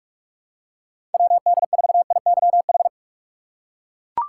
Morse code, heard as a keyed tone, about 41 wpm, 700 Hz. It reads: WD4IYH
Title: USA Call Signs - 6 Characters - 3x Character Spacing - 45wpm